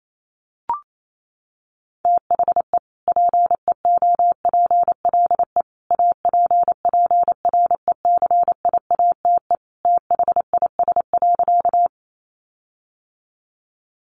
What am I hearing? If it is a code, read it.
Morse code, heard as a keyed tone, about 28 wpm, 700 Hz. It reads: THE PEOPLE APPRECIATE THIS.
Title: Sentences from Top 1300 Words 28wpm